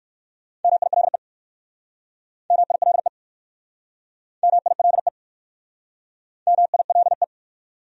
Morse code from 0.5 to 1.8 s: MILE

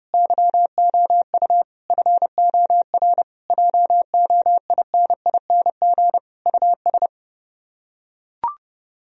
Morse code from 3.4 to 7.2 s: JOINING US

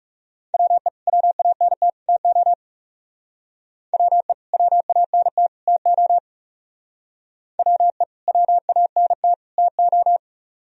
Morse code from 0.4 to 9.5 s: WE WANT TO WE WANT TO WE WANT

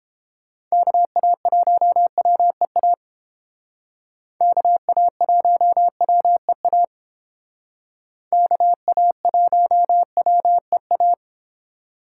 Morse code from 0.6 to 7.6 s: KA1WEA KA1WEA

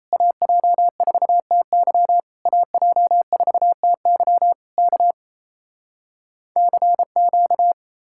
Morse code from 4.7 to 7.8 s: K CQ

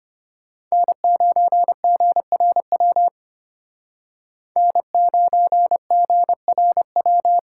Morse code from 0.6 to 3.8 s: N9GRW

